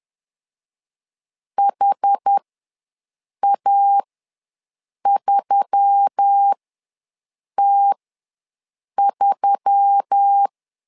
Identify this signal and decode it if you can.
Morse code, heard as a keyed tone, about 10 wpm, spaced as for 6 wpm, 800 Hz. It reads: HA3T3
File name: LEZIONI / 4 Q1S3 ws 10wpm 2437